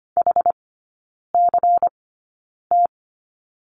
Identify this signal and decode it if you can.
Morse code, heard as a keyed tone, about 25 wpm, spaced as for 9 wpm, 700 Hz. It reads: HCT
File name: course / 25:10wpm / 093 Lesson 16e - Letter c - Three Characters 25wpm